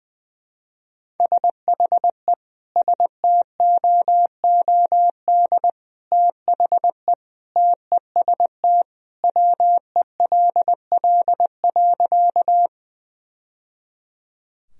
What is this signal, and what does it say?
Morse code, heard as a keyed tone, about 20 wpm, 700 Hz. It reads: SHE STOOD THE TEST WELL.
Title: Sentences from Top 1000 Words - Review Part 3 - 20wpm